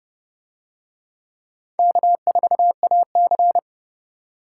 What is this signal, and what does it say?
Morse code, heard as a keyed tone, about 30 wpm, 700 Hz. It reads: K4AC